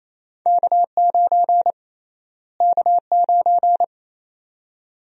Morse code, heard as a keyed tone, about 28 wpm, 700 Hz. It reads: K9 K9
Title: USA Call Sign Prefixes - Repeated 2x - 28wpm